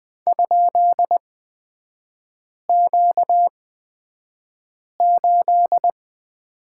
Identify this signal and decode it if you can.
Morse code, heard as a keyed tone, about 20 wpm, spaced as for 5 wpm, 700 Hz. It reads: ?Q8
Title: 236 Lesson 38e - Letter q - Three Characters 20wpm